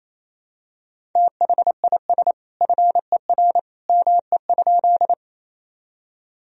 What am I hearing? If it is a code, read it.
Morse code, heard as a keyed tone, about 28 wpm, 700 Hz. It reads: THIS FER ME?
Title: Sentences from Top 200 Words and Abbreviations 28wpm